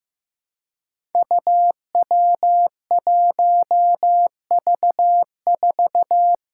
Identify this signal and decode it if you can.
Morse code, heard as a keyed tone, about 15 wpm, 700 Hz. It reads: UW1V4
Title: Five Challenging Letter-Number Combinations 15wpm